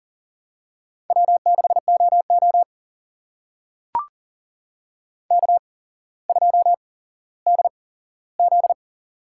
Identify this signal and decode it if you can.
Morse code, heard as a keyed tone, about 38 wpm, spaced as for 17 wpm, 700 Hz. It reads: W6OO K2DZ
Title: USA Call Signs - 4 Characters - 8x Character Spacing - 40wpm